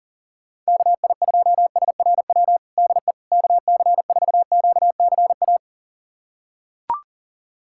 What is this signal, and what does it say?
Morse code, heard as a keyed tone, about 40 wpm, 700 Hz. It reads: KI2SRW DE KC4QCA
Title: USA Callsign DE Callsign 40wpm